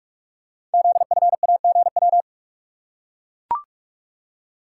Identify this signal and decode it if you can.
Morse code, heard as a keyed tone, about 44 wpm, 700 Hz. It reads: ZFAGW
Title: Code Groups - 5 Letters - 45wpm